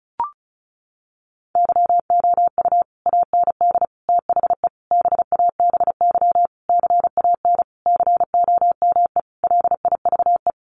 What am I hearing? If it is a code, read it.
Morse code, heard as a keyed tone, about 34 wpm, 700 Hz. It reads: YOU AND THE BABY CAN COME LIVE
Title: Sentences from Top 1300 Words 35wpm